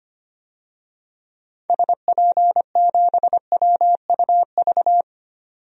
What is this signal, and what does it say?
Morse code, heard as a keyed tone, about 25 wpm, 700 Hz. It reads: SP7WUV
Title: DXCC-100 Call Signs - 6 Characters - 25wpm